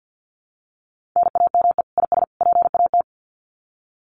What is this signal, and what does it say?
Morse code, heard as a keyed tone, about 50 wpm, 700 Hz. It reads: NAME IS PAT
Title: Common QSO Phrases 50wpm